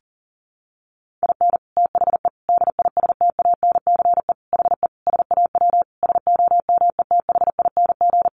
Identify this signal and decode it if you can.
Morse code, heard as a keyed tone, about 37 wpm, 700 Hz. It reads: IN THE DISTANCE HE SAW SOMETHING
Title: Sentences from Top 1200 Words 40wpm